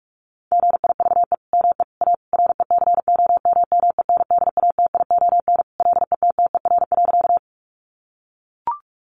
Morse code, heard as a keyed tone, about 44 wpm, 700 Hz. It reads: GIVE ME A RECOMMENDATION LETTER.
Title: Sentences from Top 1400 Words 45wpm